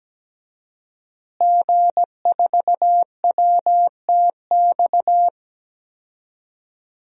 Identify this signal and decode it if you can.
Morse code, heard as a keyed tone, about 17 wpm, 700 Hz. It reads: G4WTX
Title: DXCC-100 Call Signs - 5 Characters - Repeated 3x - 17wpm